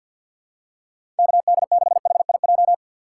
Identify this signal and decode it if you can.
Morse code, heard as a keyed tone, about 52 wpm, 700 Hz. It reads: KD6HIJ
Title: USA Call Signs - 6 Characters - 4x Character Spacing - 50wpm